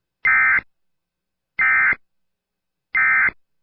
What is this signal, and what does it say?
Emergency Alert System (SAME) tones, heard as harsh buzzing bursts. It is heard from 0.2 to 3.3 s.